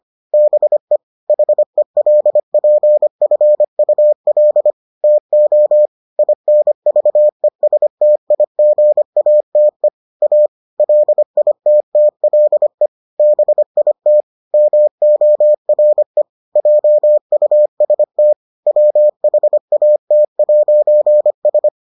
Morse code, heard as a keyed tone, about 25 wpm, 600 Hz. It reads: BE HELPFUL TO INVESTIGATE A LITTLE BIT MORE JUST WHAT'S